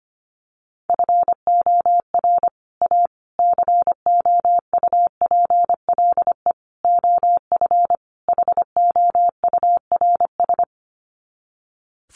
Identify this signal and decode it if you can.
Morse code, heard as a keyed tone, about 25 wpm, 700 Hz. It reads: FOR A COUPLE OF HOURS
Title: Sets of 5 Words Encore2 25wpm